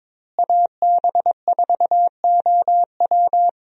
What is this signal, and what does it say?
Morse code, heard as a keyed tone, about 21 wpm, 700 Hz. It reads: AB4OW
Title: USA Call Signs - 5 Characters - Repeated 3x - 22wpm